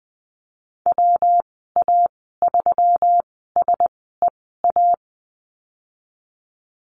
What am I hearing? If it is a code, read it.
Morse code, heard as a keyed tone, about 19 wpm, spaced as for 14 wpm, 700 Hz. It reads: WA3SEA
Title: 122 Lesson 20g - Number 3 - Call Signs 20wpm